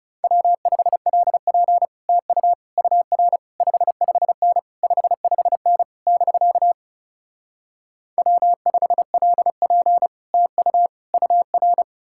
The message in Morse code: W5LP TU UR 55N 55N <BK> W5LP TU UR

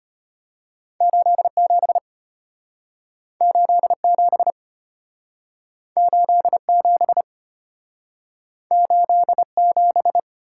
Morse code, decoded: 87 87 87 87

